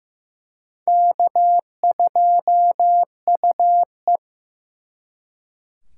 Morse code, heard as a keyed tone, about 18 wpm, 700 Hz. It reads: K2UE
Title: Speed Racing - USA 1x2 Call Signs 15wpm